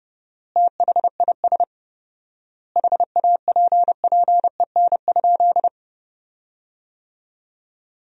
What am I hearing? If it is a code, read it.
Morse code, heard as a keyed tone, about 30 wpm, 700 Hz. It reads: THIS HAPPEN?
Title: 4x Word-Spacing - Sentences from Top 400 Words - 30wpm